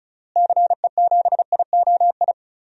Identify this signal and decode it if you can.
Morse code, heard as a keyed tone, about 34 wpm, 700 Hz. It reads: CE7IOI
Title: DXCC-100 Call Signs - 6 Characters - Repeated 3x - 35wpm